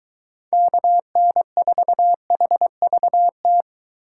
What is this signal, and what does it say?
Morse code, heard as a keyed tone, about 23 wpm, 700 Hz. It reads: KN4HVT